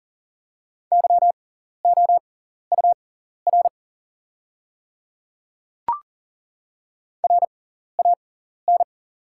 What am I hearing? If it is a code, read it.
Morse code, heard as a keyed tone, about 40 wpm, spaced as for 14 wpm, 700 Hz. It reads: YOUR RAN
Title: 6x Character Spacing - Top 500 Words - 40wpm